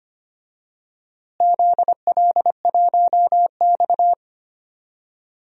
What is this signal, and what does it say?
Morse code, heard as a keyed tone, about 25 wpm, 700 Hz. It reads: ZL1X
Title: DXCC-100 Call Signs - 4 Characters - Repeated 2x - 25wpm